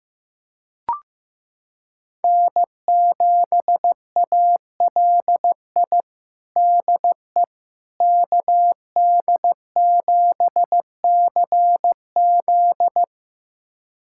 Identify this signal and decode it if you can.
Morse code, heard as a keyed tone, about 15 wpm, 700 Hz. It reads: N7ALI DE KD7CZ
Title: USA Callsign DE Callsign - 5 Characters - 15wpm